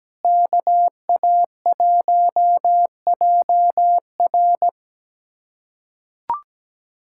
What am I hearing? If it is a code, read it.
Morse code, heard as a keyed tone, about 17 wpm, 700 Hz. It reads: KA1JR
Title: USA Callsign DE Callsign - 5 Characters - 17wpm